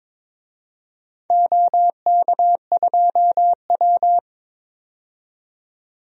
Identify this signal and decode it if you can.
Morse code, heard as a keyed tone, about 22 wpm, 700 Hz. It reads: OK2W
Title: DXCC-100 Call Signs - 4 Characters - 22wpm